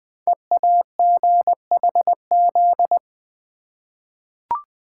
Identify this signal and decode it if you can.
Morse code, heard as a keyed tone, about 20 wpm, 700 Hz. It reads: EAGHZ
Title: Code Groups - 5 Letters - 20wpm